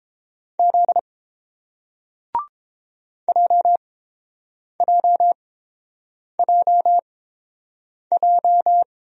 Morse code, heard as a keyed tone, about 30 wpm, spaced as for 7 wpm, 700 Hz. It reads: Z JJJJ